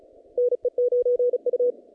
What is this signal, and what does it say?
Morse code, heard as a keyed tone, about 36 wpm, 490 Hz. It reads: NE9U
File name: contests / naqp-cw-k5tr-op-k5pi- -2100